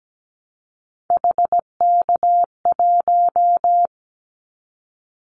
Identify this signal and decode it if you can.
Morse code, heard as a keyed tone, about 17 wpm, 700 Hz. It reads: HK1
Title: Callsign Prefixes for Top 100 DXCC - 17wpm